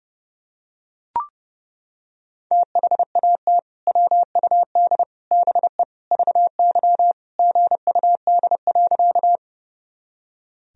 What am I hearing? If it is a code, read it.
Morse code, heard as a keyed tone, about 30 wpm, 700 Hz. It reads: THAT WUD BE VY GUD.
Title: Sentences from Top 200 Words and Abbreviations 30wpm